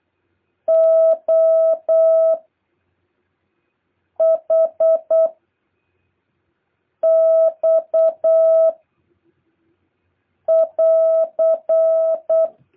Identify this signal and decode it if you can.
Morse code, heard as a keyed tone, about 8 wpm, 650 Hz. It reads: O H X <AR>
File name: jh8cbh / 2023cw